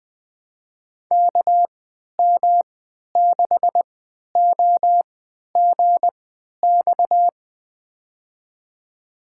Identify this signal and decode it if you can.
Morse code, heard as a keyed tone, about 20 wpm, spaced as for 11 wpm, 700 Hz. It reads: KM6OGX
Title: USA Call Signs - 6 Characters - 3x Character Spacing - 20wpm